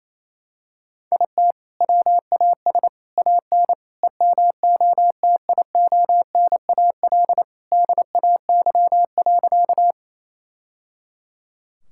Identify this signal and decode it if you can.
Morse code, heard as a keyed tone, about 28 wpm, 700 Hz. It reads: IT WAS AN EMOTIONAL DAY.
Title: Sentences from Top 1000 Words - Review Part 5 - 28wpm